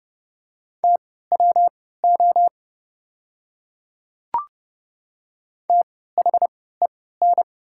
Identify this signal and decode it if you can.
Morse code, heard as a keyed tone, about 32 wpm, spaced as for 17 wpm, 700 Hz. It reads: TWO THEN
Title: 3x Character Spacing - Top 200 Words - 30wpm